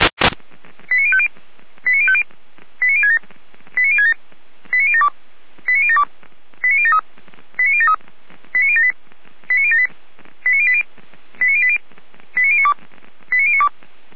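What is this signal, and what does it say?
Five-tone selective call (ZVEI), heard as quick runs of five tones, tones of about 70 ms. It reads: 89050 89050 89067 89067 89072 89072 89073 89073 89078 89078 89080 89080 89092 89092
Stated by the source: Five-tone ZVEI selective call 89022 received by SDR, from audio.f85360.feuerwehr-orlamuende.org fire brigade server.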